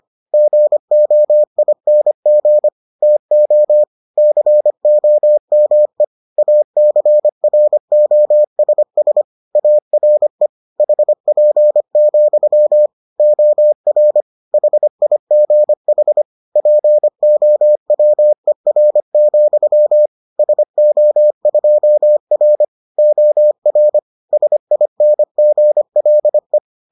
Morse code, heard as a keyed tone, about 25 wpm, 600 Hz. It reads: GOING TO COME ACROSS ARE HP, OR HIGH POWER, SO2R OR SINGLE